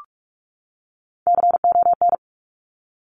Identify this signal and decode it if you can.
Morse code, heard as a keyed tone, about 46 wpm, 700 Hz. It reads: CON
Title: Top Three Letter Combinations in Words 45wpm